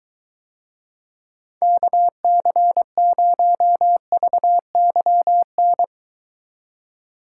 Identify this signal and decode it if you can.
Morse code, heard as a keyed tone, about 24 wpm, 700 Hz. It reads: KC0VYN